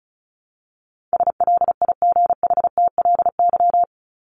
Morse code, heard as a keyed tone, about 34 wpm, 700 Hz. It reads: SLIGHTLY
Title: LY Suffix - Top 100 Words with Base Words 35wpm